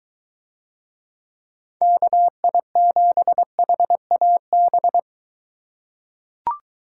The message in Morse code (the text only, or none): KI7HAB